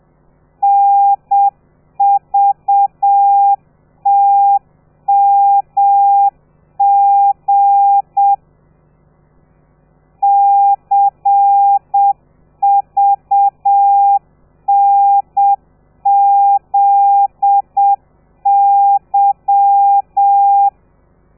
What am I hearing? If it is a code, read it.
Morse code, heard as a keyed tone, about 7 wpm, 800 Hz. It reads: NVTMG CVNZY
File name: telegrafia / i2tfj / Lez13 30